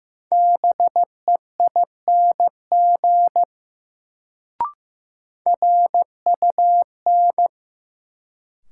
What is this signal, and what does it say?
Morse code, heard as a keyed tone, about 15 wpm, 700 Hz. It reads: BEING RUN
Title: Sets of 4 Words - New Word Drill - 15wpm